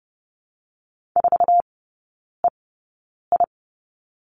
Morse code, heard as a keyed tone, about 30 wpm, spaced as for 9 wpm, 700 Hz. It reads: 4EI
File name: Lesson 17e - Letter u - Three Characters 30wpm